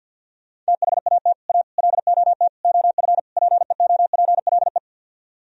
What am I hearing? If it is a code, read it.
Morse code, heard as a keyed tone, about 64 wpm, 700 Hz. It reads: THAT A LOT OF PEOPLE